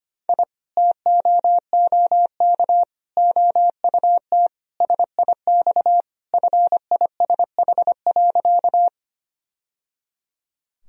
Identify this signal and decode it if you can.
Morse code, heard as a keyed tone, about 25 wpm, 700 Hz. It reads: I TOOK OUT SIX FISH.
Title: Sentences from Top 1000 Words - Review Part 2 - 25wpm